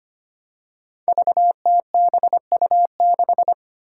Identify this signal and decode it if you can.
Morse code, heard as a keyed tone, about 25 wpm, 700 Hz. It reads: VTBU6
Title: Five Challenging Letter-Number Combinations 25wpm